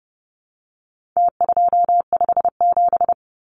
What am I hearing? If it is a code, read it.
Morse code, heard as a keyed tone, about 30 wpm, 700 Hz. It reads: T257
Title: Numbers - Contest Style - 4-Digit 30wpm